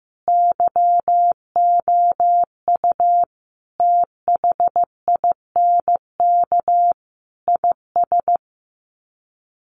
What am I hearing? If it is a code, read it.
Morse code, heard as a keyed tone, about 15 wpm, 700 Hz. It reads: YOU THINK IS
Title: Sets of 5 Words Encore 15wpm